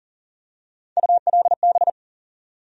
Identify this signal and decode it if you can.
Morse code, heard as a keyed tone, about 40 wpm, 700 Hz. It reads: ULB
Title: USA Call Sign Postfixes - 3 Letters 40wpm